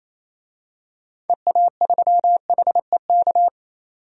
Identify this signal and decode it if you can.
Morse code, heard as a keyed tone, about 28 wpm, 700 Hz. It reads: EA3HEK
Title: DXCC-100 Call Signs - 6 Characters - 28wpm